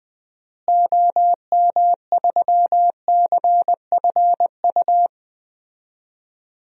Morse code, heard as a keyed tone, about 20 wpm, 700 Hz. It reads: OM3CFU
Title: DXCC-100 Call Signs - 6 Characters - Repeated 3x - 20wpm